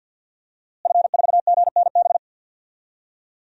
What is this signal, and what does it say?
Morse code, heard as a keyed tone, about 48 wpm, 700 Hz. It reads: F4GNB
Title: DXCC-100 Call Signs - 5 Characters - Repeated 2x - 50wpm